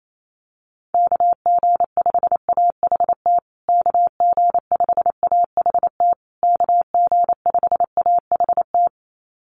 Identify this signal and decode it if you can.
Morse code, heard as a keyed tone, about 28 wpm, 700 Hz. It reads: KG5AHT KG5AHT KG5AHT
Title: USA Call Signs Repeated 3x 28wpm